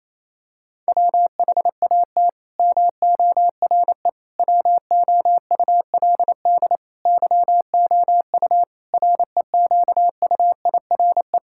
Morse code, heard as a keyed tone, about 28 wpm, 700 Hz. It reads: WHAT MORE WOULD YOU REQUIRE
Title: Sentences from Top 1000 Words - Review Part 6 - 28wpm